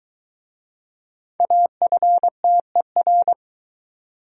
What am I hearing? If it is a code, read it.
Morse code, heard as a keyed tone, about 23 wpm, 700 Hz. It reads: AFTER